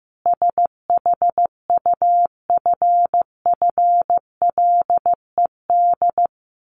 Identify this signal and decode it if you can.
Morse code, heard as a keyed tone, about 15 wpm, 700 Hz. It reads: SHUFFLED
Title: A Sound Of Thunder 15wpm